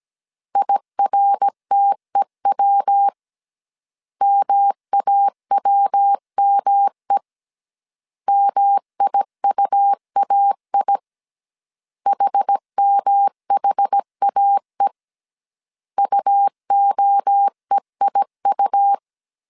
Morse code, heard as a keyed tone, about 17 wpm, 800 Hz. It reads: IRTEW MAWME MIUAI HMHAE UOEIU